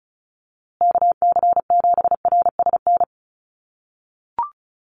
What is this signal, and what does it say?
Morse code, heard as a keyed tone, about 34 wpm, 700 Hz. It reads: KC7RSN
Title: USA Call Signs Repeated 2x 35wpm